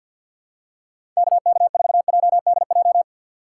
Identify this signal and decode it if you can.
Morse code, heard as a keyed tone, about 51 wpm, 700 Hz. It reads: KK4JDJ
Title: USA Call Signs - 6 Characters - 4x Character Spacing - 50wpm